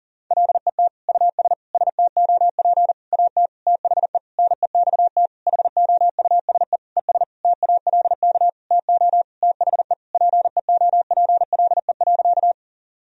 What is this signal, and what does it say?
Morse code, heard as a keyed tone, about 37 wpm, 700 Hz. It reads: LET US STOP AT THE NEXT HOUSE ES TALK TO THE PEOPLE.
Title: Sentences from Top 400 Words and Abbreviations 40wpm